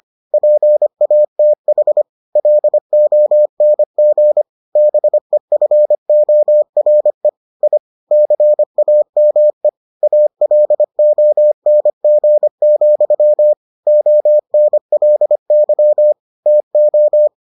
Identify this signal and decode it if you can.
Morse code, heard as a keyed tone, about 25 wpm, 600 Hz. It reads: PATH LONG BEFORE I CAME ALONG, ONLY TO